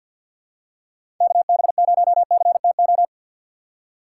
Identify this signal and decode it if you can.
Morse code, heard as a keyed tone, about 48 wpm, 700 Hz. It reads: KB0CTO
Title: USA Call Signs - 6 Characters - 6x Character Spacing - 50wpm